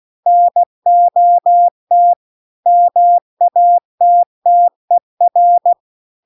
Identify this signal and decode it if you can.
Morse code, heard as a keyed tone, about 16 wpm, 700 Hz. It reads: NOT MATTER